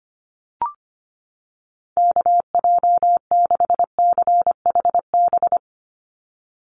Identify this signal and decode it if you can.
Morse code, heard as a keyed tone, about 25 wpm, 700 Hz. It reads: KJ6CHB
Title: USA Call Signs 25wpm